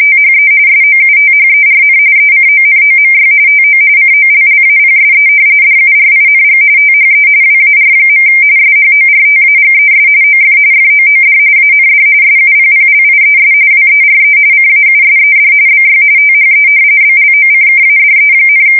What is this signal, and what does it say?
RTTY, about 45.45 baud. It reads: THE SOUTHERN ECONOMY TO COLLAPSE DURING THE WAR. SHORTAGES OF FOOD AND SUPPLIES WERE CAUSED BY THE BLOCKADE,
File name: fsk170 / usnavy3